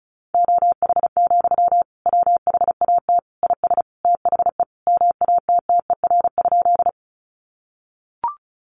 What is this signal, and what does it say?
Morse code, heard as a keyed tone, about 34 wpm, 700 Hz. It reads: OH, WHAT IS THE MATTER?
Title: Sentences from Top 700 Words 35wpm